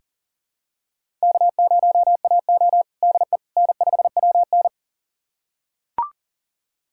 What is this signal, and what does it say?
Morse code, heard as a keyed tone, about 40 wpm, 700 Hz. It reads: K0AO DE N5WN
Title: USA Callsign DE Callsign - 4 Characters - 40wpm